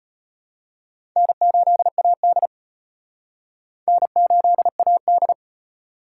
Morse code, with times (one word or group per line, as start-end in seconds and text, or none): 1.2-2.5 N8AD
3.9-5.3 N8AD